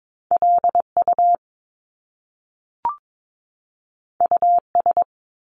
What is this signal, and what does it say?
Morse code, heard as a keyed tone, about 22 wpm, 700 Hz. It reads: LU US
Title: Top Two Letter Combinations in Words 22wpm